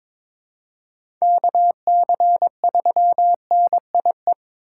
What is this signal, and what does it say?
Morse code, heard as a keyed tone, about 22 wpm, 700 Hz. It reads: KC3NIE
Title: K1USN SST QSO Practice - 22wpm